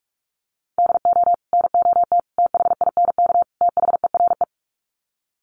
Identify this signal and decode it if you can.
Morse code, heard as a keyed tone, about 44 wpm, 700 Hz. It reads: DO NOT THINK THERE
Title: Sets of 4 Words 45wpm